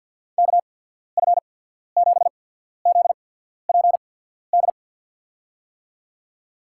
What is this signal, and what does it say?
Morse code, heard as a keyed tone, about 53 wpm, spaced as for 14 wpm, 700 Hz. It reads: KF7ZPD
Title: USA Call Signs - 6 Characters - 8x Character Spacing - 50wpm